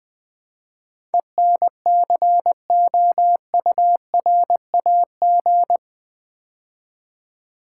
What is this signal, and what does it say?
Morse code, heard as a keyed tone, about 20 wpm, 700 Hz. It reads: ENCOURAG
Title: Build A Word - 4 Letters At A Time - 11-Letter Words - 20wpm